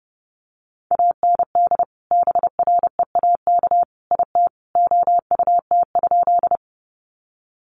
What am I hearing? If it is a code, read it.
Morse code, heard as a keyed tone, about 30 wpm, 700 Hz. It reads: AND BREAK IT OUT?
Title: Sentences from Top 900 Words 30wpm